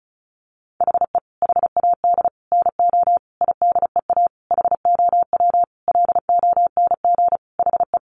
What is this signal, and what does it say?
Morse code, heard as a keyed tone, about 34 wpm, 700 Hz. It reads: HE HAD NO IDEA HOW LONG HE
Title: Sentences from Top 300 Words 35wpm